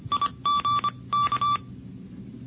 Morse code, heard as a keyed tone, about 25 wpm, 1160 Hz. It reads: IG K
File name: NCCC sweepstakes